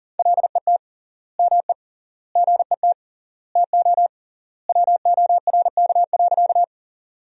Morse code, heard as a keyed tone, about 40 wpm, 700 Hz. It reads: LET ME GET TO WORK.
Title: 3x Word-Spacing - Sentences from Top 300 Words - 40wpm